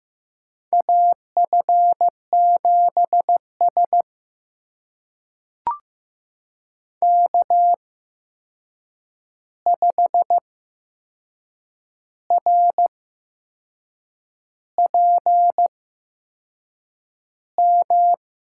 Morse code, heard as a keyed tone, about 15 wpm, spaced as for 6 wpm, 700 Hz. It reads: AF7S K5RPM
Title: USA Call Signs - 8x Character Spacing - 15wpm